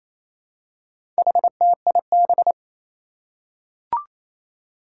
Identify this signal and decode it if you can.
Morse code, heard as a keyed tone, about 28 wpm, 700 Hz. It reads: HTIB
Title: Four Challenging Letter-Number Combinations 28wpm